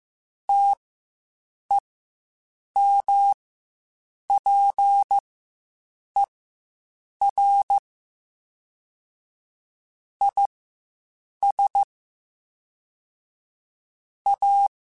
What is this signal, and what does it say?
Morse code, heard as a keyed tone, about 15 wpm, spaced as for 7 wpm, 800 Hz. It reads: TEMPER IS A
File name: Quotes Spaced / Quotes Spaced-mp3s / Temper sp @10